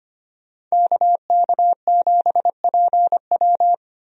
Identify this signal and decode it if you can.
Morse code, heard as a keyed tone, about 25 wpm, 700 Hz. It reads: KK7PW
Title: USA Call Signs - 5 Characters - 25wpm